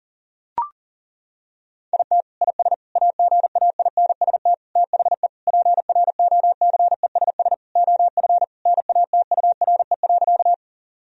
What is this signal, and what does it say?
Morse code, heard as a keyed tone, about 40 wpm, 700 Hz. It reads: IT IS AGAINST THE PROCESS OF NATURE.
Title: Sentences from Top 1000 Words - Review Part 3 - 40wpm